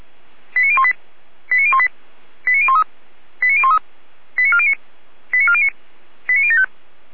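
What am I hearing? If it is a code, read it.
Five-tone selective call (ZVEI), heard as quick runs of five tones, tones of about 70 ms. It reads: 89018 89018 89013 89013 89409 89409 89075
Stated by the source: ZVEI five-tone call received by SDR.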